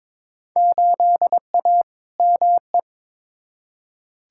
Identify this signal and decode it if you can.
Morse code, heard as a keyed tone, about 22 wpm, 700 Hz. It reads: 8A ME
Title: Field Day Exchanges - 22wpm